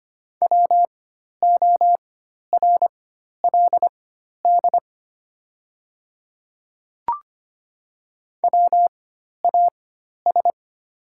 Morse code, heard as a keyed tone, about 25 wpm, spaced as for 12 wpm, 690 Hz. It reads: WORLD WAS
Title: 4x Character Spacing - Top 300 Words - 25wpm